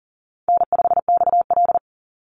Morse code, heard as a keyed tone, about 40 wpm, 700 Hz. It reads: N5XL
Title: NA SOTA Chaser Callsigns 40wpm